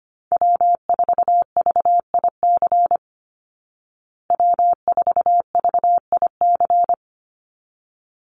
Morse code, heard as a keyed tone, about 25 wpm, 700 Hz. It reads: W4VIC W4VIC
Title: Call Signs Repeated 3x - 25wpm